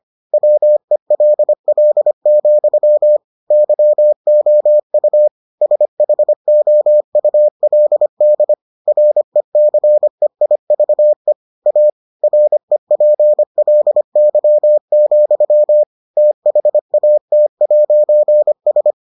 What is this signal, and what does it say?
Morse code, heard as a keyed tone, about 25 wpm, 600 Hz. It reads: WELL, YOU SHOULD RECEIVE A REPLY, THAT'S